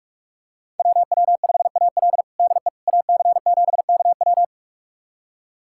Morse code, heard as a keyed tone, about 46 wpm, 700 Hz. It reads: WW5AL DE AC7KW
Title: USA Callsign DE Callsign - 5 Characters - 45wpm